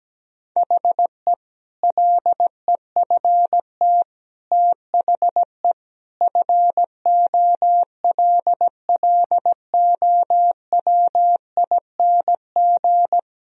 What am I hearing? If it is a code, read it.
Morse code, heard as a keyed tone, about 17 wpm, 700 Hz. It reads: HE LEFT THE FOLLOWING
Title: Sentences from Top 1000 Words - Review Part 4 - 17wpm